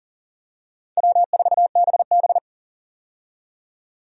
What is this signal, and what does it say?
Morse code, heard as a keyed tone, about 40 wpm, 700 Hz. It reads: W4BB
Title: USA Call Signs - 4 Characters - 40wpm